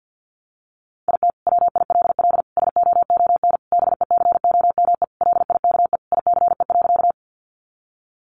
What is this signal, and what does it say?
Morse code, heard as a keyed tone, about 50 wpm, 700 Hz. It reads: IT WILL SOON BECOME LIKE ICE.
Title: Sentences from Top 1000 Words 50wpm